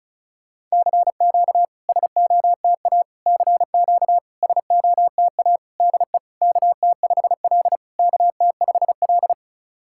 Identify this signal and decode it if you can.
Morse code, heard as a keyed tone, about 36 wpm, 690 Hz. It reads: CQ SOTA CQ SOTA DE KT5L KT5L
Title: SOTA QSO Practice 35wpm